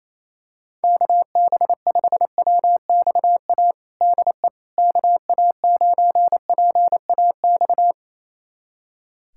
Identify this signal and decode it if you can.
Morse code, heard as a keyed tone, about 28 wpm, 700 Hz. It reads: KB5WXA DE KA9PAX
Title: USA Callsign DE Callsign - 6 Characters - 28wpm